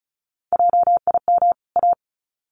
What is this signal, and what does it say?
Morse code, heard as a keyed tone, about 34 wpm, 700 Hz. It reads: JIM A
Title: CWT Exchange 35wpm